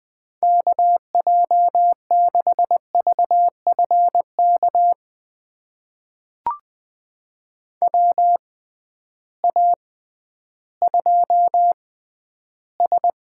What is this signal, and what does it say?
Morse code, heard as a keyed tone, about 20 wpm, spaced as for 12 wpm, 700 Hz. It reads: KJ6VFK WA2S
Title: USA Call Signs - 6x Character Spacing - 20wpm